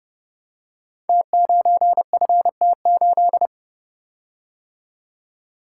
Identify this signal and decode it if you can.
Morse code, heard as a keyed tone, about 30 wpm, 700 Hz. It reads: T9FT8
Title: Code Groups - 5 Letter:Numbers - 30wpm